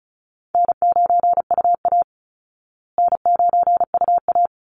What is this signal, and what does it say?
Morse code, heard as a keyed tone, about 34 wpm, 700 Hz. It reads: N9UA N9UA
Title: Easy Call Signs Repeated 2x - 35wpm